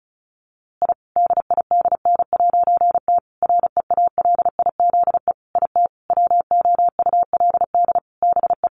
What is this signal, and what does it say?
Morse code, heard as a keyed tone, about 34 wpm, 700 Hz. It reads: I DIDN'T REALIZE IT WOULD BE
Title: A Sound Of Thunder 35wpm